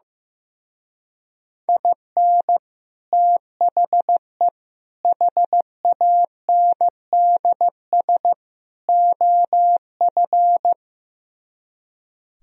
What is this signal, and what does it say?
Morse code, heard as a keyed tone, about 15 wpm, 700 Hz. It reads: IN THE HANDS OF
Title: Speed-Racing - Sets of 4 Words 15wpm